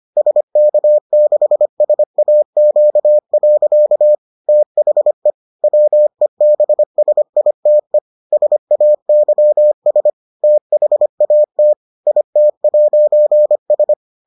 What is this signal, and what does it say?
Morse code, heard as a keyed tone, about 25 wpm, 600 Hz. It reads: SK6SAQ. THE WEBSITE SAYS THAT IT'S